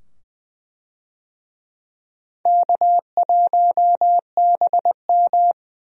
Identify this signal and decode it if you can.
Morse code, heard as a keyed tone, about 20 wpm, 700 Hz. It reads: K1BM
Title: USA Call Signs - 4 Characters - 20wpm